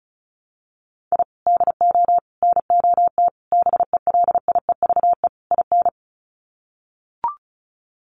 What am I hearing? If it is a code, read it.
Morse code, heard as a keyed tone, about 34 wpm, 700 Hz. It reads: I DO NOT BELIEVE IN
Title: Sets of 5 Words Encore 35wpm